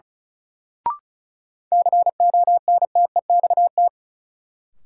Morse code, heard as a keyed tone, about 36 wpm, 700 Hz. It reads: CONTEXT
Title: Sets of 4 Words - New Word Drill - 35wpm